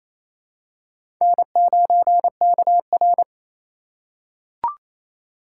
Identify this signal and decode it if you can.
Morse code, heard as a keyed tone, about 28 wpm, 700 Hz. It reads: N9KR
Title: K1USN SST QSO Practice - 28wpm